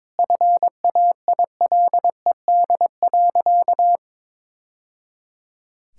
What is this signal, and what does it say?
Morse code, heard as a keyed tone, about 22 wpm, 700 Hz. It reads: FAILED.
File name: practice / 22wpm / Sentences with 2-Word Phrases - 22wpm